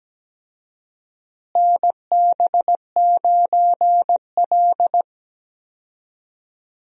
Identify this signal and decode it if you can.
Morse code, heard as a keyed tone, about 17 wpm, 700 Hz. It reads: NB9L
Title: USA Call Signs - 4 Characters - Repeated 2x - 17wpm